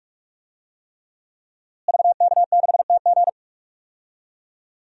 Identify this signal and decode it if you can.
Morse code, heard as a keyed tone, about 46 wpm, 700 Hz. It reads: VK6TG
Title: DXCC-100 Call Signs - 5 Characters - Repeated 3x - 45wpm